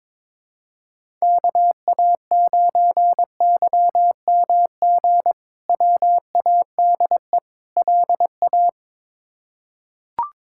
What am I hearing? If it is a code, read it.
Morse code, heard as a keyed tone, about 22 wpm, 700 Hz. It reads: KA9YMG WADE LA
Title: K1USN SST QSO Practice - 22wpm